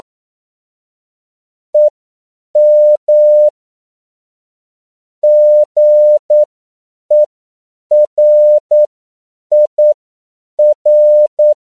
Morse code, heard as a keyed tone, about 9 wpm, spaced as for 7 wpm, 600 Hz. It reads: EM GERIR
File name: CW / 10PPM